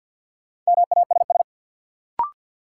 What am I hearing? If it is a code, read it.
Morse code, heard as a keyed tone, about 53 wpm, 700 Hz. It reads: MASS
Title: Sets of 5 Words Encore - New Word Drill - 50wpm